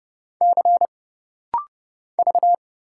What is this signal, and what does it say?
Morse code, heard as a keyed tone, about 30 wpm, 700 Hz. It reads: C V
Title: Single Letters - Rapid-Fire 30wpm